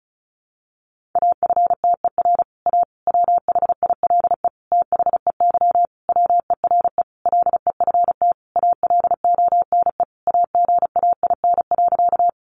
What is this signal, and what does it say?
Morse code, heard as a keyed tone, about 34 wpm, 700 Hz. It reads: AFTER A WHILE THEY WERE LEFT ALONE AGAIN.
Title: Sentences from Top 1000 Words 35wpm